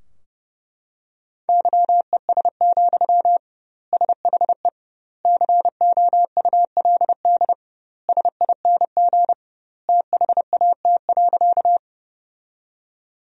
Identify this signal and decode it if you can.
Morse code, heard as a keyed tone, about 30 wpm, 700 Hz. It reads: YES, SHE COULD SING THAT.
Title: 2x Word-Spacing - Sentences from Top 500 Words - 30wpm